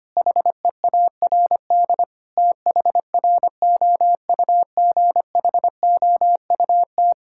Morse code, heard as a keyed tone, about 25 wpm, 700 Hz. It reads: HEARD THROUGHOUT
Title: Sentences from Top 1000 Words - Review Part 5 - 25wpm